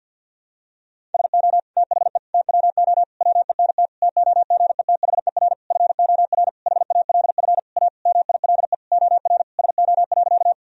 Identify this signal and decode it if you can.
Morse code, heard as a keyed tone, about 49 wpm, 700 Hz. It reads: SO THE TWO WENT TOGETHER FOR HALF A MILE OR SO.
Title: Sentences from Top 1000 Words - Review Part 1 - 50wpm